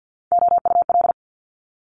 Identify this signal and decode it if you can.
Morse code, heard as a keyed tone, about 50 wpm, 700 Hz. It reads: OUL
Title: Top Three Letter Combinations in Words 50wpm